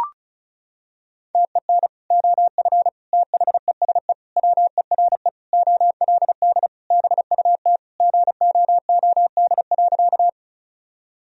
Morse code, heard as a keyed tone, about 34 wpm, 700 Hz. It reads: TEN OF THESE WERE OLD BUT GOOD.
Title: Sentences from Top 1000 Words - Review Part 2 - 35wpm